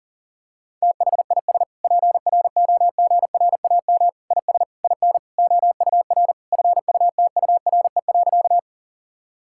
Morse code, heard as a keyed tone, about 40 wpm, 700 Hz. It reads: THIS PROGRAM IS IN OUR FUTURE.
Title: Sentences from Top 1000 Words - Review Part 5 - 40wpm